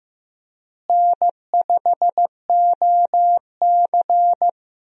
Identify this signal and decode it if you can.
Morse code, heard as a keyed tone, about 16 wpm, 700 Hz. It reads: N5OC
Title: USA Call Signs - 4 Characters - Repeated 2x - 15wpm